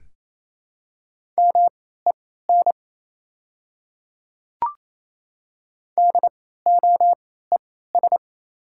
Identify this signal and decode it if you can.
Morse code, heard as a keyed tone, about 28 wpm, spaced as for 16 wpm, 700 Hz. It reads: MEN DOES
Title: 3x Character Spacing - Top 200 Words - 28wpm